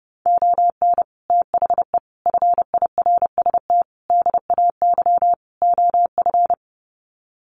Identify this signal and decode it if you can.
Morse code, heard as a keyed tone, about 30 wpm, 700 Hz. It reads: ON THE FIRST DAY OF
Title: Sets of 5 Words Encore 30wpm